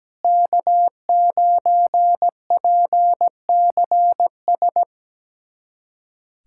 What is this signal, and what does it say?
Morse code, heard as a keyed tone, about 17 wpm, 700 Hz. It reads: K9PCS
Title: USA Callsign DE Callsign - 5 Characters - 17wpm